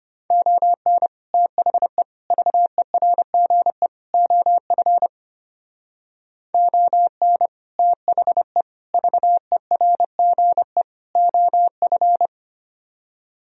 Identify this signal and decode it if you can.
Morse code, heard as a keyed tone, about 32 wpm, 700 Hz. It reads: ON THE VERGE OF ON THE VERGE OF